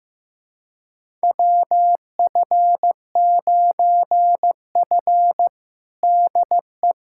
Morse code, heard as a keyed tone, about 16 wpm, 700 Hz. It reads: WF9F DE
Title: USA Callsign DE Callsign - 4 Characters - 15wpm